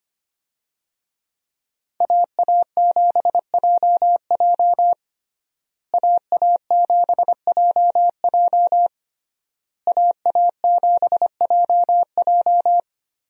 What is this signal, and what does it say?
Morse code, heard as a keyed tone, about 25 wpm, 700 Hz. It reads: AA7JJ AA7JJ AA7JJ
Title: USA Call Signs - 5 Characters - Repeated 3x - 25wpm